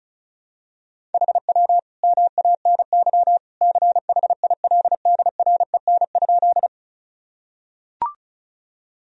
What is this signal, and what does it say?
Morse code, heard as a keyed tone, about 36 wpm, 700 Hz. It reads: HW MANY CHILDREN?
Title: Sentences from Top 300 Words and Abbreviations 35wpm